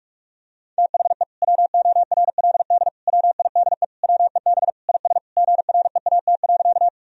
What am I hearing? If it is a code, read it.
Morse code, heard as a keyed tone, about 46 wpm, 700 Hz. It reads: THE WORLD WIDE WEB IS GREAT.
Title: Sentences from Top 1000 Words - Review Part 4 - 45wpm